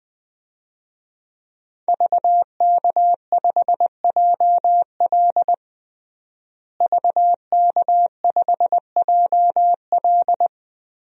Morse code, heard as a keyed tone, about 21 wpm, 700 Hz. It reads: VK5JL VK5JL
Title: DXCC-100 Call Signs - 5 Characters - Repeated 3x - 20wpm